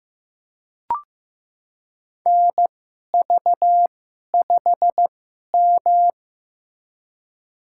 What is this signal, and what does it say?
Morse code, heard as a keyed tone, about 15 wpm, spaced as for 11 wpm, 700 Hz. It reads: NV5M
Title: USA Call Signs - 4 Characters - 2x Character Spacing - 15wpm